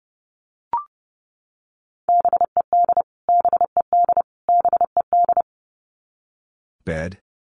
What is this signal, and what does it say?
Morse code, heard as a keyed tone, about 28 wpm, 700 Hz. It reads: BED BED BED
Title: Top 500 Words Repeated 30wpm